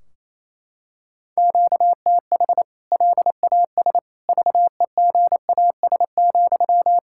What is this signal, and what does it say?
Morse code, heard as a keyed tone, about 28 wpm, 700 Hz. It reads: QTH LAS VEGAS,
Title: K1USN SST QSO Practice 28wpm